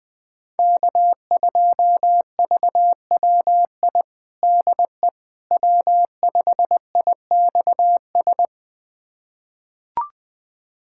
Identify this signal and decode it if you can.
Morse code, heard as a keyed tone, about 20 wpm, 700 Hz. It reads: K2VWI DE W5IXS
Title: USA Callsign DE Callsign - 5 Characters - 20wpm